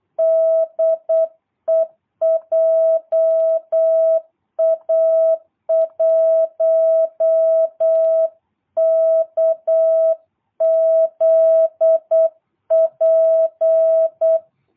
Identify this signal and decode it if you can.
Morse code, heard as a keyed tone, about 8 wpm, 660 Hz. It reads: DE JA1KZP
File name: jh8cbh / 2022cw